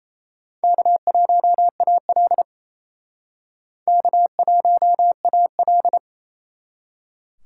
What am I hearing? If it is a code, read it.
Morse code, heard as a keyed tone, about 37 wpm, 700 Hz. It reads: K1AL K1AL